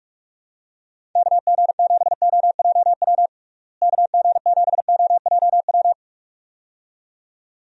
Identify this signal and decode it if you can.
Morse code, heard as a keyed tone, about 46 wpm, 700 Hz. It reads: KG7OJW KG7OJW
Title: USA Call Signs - 6 Characters - Repeated 2x - 45wpm